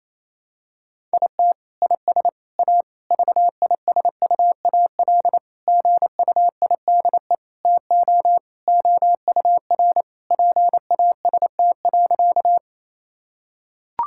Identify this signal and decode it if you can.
Morse code, heard as a keyed tone, about 28 wpm, 700 Hz. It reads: IT IS A VISUAL GUIDE TO OUR PAST.
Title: Sentences from Top 1000 Words - Review Part 6 - 28wpm